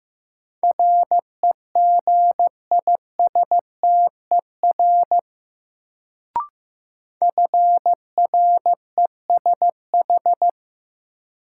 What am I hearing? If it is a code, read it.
Morse code, heard as a keyed tone, about 15 wpm, 700 Hz. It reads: REGISTER FRESH
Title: Sentences from Top 1200 Words - New Word Drill - 15wpm